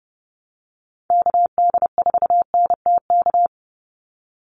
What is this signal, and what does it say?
Morse code, heard as a keyed tone, about 30 wpm, 700 Hz. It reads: KD4NTK
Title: USA Call Signs 30wpm